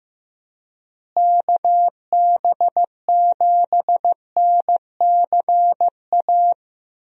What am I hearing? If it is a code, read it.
Morse code, heard as a keyed tone, about 15 wpm, 700 Hz. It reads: KB7NCA
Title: USA Call Signs - 6 Characters - Repeated 2x - 15wpm